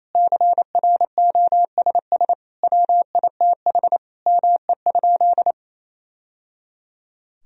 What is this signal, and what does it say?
Morse code, heard as a keyed tone, about 28 wpm, 700 Hz. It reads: CROSS WITH ME?
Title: Sentences from Top 1000 Words - Review Part 1 - 28wpm